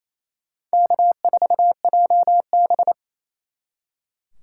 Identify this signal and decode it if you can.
Morse code, heard as a keyed tone, about 35 wpm, 700 Hz. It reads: K4JB